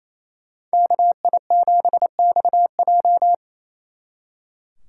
Morse code, heard as a keyed tone, about 33 wpm, 700 Hz. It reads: KI7XJ
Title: Speed Racing - USA 2x2 Call Signs 28wpm